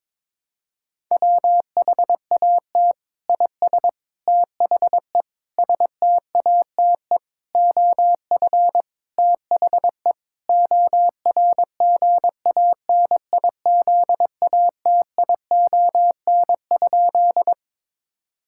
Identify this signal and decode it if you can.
Morse code, heard as a keyed tone, about 22 wpm, 700 Hz. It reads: WHAT IS THE STATE OF THE ORGANIZATION?
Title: Sentences from Top 1000 Words - Review Part 5 - 22wpm